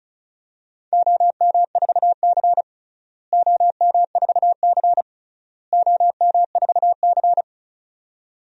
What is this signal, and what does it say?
Morse code, heard as a keyed tone, about 36 wpm, 700 Hz. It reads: OM4C OM4C OM4C
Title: DXCC-100 Call Signs - 4 Characters - Repeated 3x - 35wpm